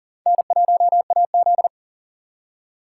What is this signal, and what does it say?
Morse code, heard as a keyed tone, about 50 wpm, 700 Hz. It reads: N1AZ
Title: Speed Racing - USA 1x2 Call Signs 40wpm